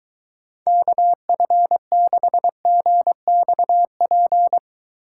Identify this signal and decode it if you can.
Morse code, heard as a keyed tone, about 23 wpm, 700 Hz. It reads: KF6GXP